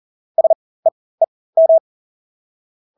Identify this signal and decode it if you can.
Morse code, heard as a keyed tone, about 40 wpm, spaced as for 20 wpm, 650 Hz. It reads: SEEM
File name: Engelske ord 40-20 wpm